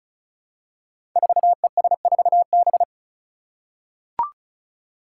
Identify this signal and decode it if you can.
Morse code, heard as a keyed tone, about 36 wpm, 700 Hz. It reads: 4ES4B
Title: Five Challenging Letter-Number Combinations 35wpm